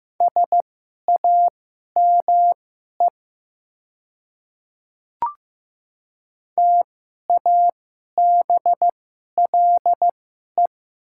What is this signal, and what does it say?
Morse code, heard as a keyed tone, about 15 wpm, spaced as for 11 wpm, 700 Hz. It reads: SAME TABLE
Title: 2x Character Spacing - Top 500 Words - 15wpm